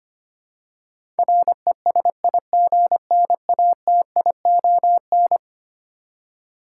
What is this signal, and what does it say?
Morse code, heard as a keyed tone, about 25 wpm, 700 Hz. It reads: RESIGNATION
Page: morse-code-ninja-practice-25wpm